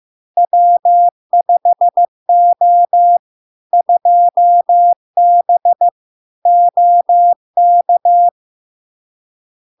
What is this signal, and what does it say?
Morse code, heard as a keyed tone, about 15 wpm, 700 Hz. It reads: W5O 2B OK
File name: LARK-FD-2022-Log-in-CW-15wpm-700-Hz